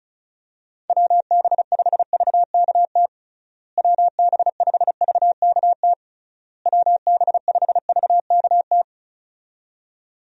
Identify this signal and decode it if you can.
Morse code, heard as a keyed tone, about 36 wpm, 700 Hz. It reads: WB5VKT WB5VKT WB5VKT
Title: USA Call Signs - 6 Characters - Repeated 3x - 35wpm